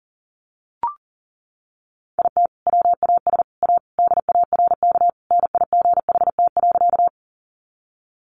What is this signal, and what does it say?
Morse code, heard as a keyed tone, about 40 wpm, 700 Hz. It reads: IT WAS A DARK NIGHT.
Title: Sentences from Top 500 Words 40wpm